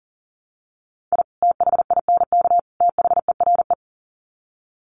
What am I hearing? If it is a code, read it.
Morse code, heard as a keyed tone, about 39 wpm, 700 Hz. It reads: I THINK THERE
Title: Sets of 3 Words 40wpm